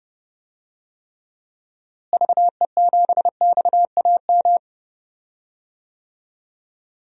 Morse code, heard as a keyed tone, about 32 wpm, 700 Hz. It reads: VE7XAM
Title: DXCC-100 Call Signs - 6 Characters - Repeated 3x - 30wpm